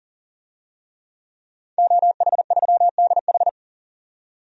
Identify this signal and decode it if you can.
Morse code, heard as a keyed tone, about 40 wpm, 700 Hz. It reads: OH3DH
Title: DXCC-100 Call Signs - 5 Characters - 40wpm